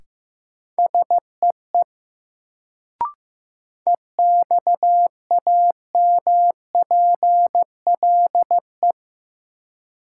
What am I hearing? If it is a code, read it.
Morse code, heard as a keyed tone, about 15 wpm, 700 Hz. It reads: SEE EXAMPLE
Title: Sets of 2 Words - New Word Drill - 15wpm